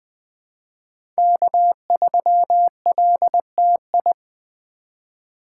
Morse code, heard as a keyed tone, about 20 wpm, 700 Hz. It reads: K3LTI
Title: USA Call Signs - 2x Character Spacing - 20wpm